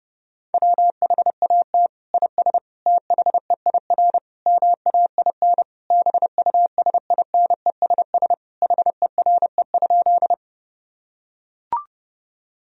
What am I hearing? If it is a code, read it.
Morse code, heard as a keyed tone, about 32 wpm, 700 Hz. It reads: WHAT IS THEIR MAIN BUSINESS HERE?
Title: Sentences from Top 1000 Words - Review Part 2 - 30wpm